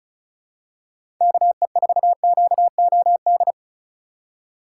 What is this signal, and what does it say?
Morse code, heard as a keyed tone, about 34 wpm, 700 Hz. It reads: KE4QOD
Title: USA Call Signs - 4x Character Spacing - 35wpm